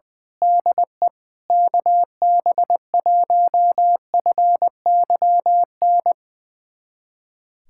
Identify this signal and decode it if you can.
Morse code, heard as a keyed tone, about 20 wpm, 700 Hz. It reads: DE KB1FYN
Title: USA Callsign DE Callsign - 6 Characters - 20wpm